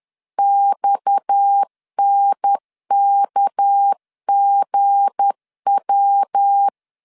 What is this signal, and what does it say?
Morse code, heard as a keyed tone, about 10 wpm, 800 Hz. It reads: XNKGW